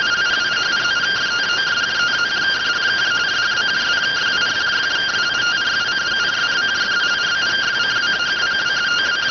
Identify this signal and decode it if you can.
Two-tone teleprinter signal, about 170 Hz shift.